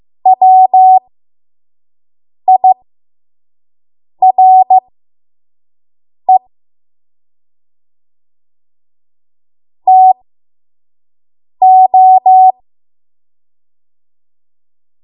Morse code, heard as a keyed tone, about 15 wpm, spaced as for 5 wpm, 750 Hz. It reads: WIRE TO
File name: Morse / 05WPM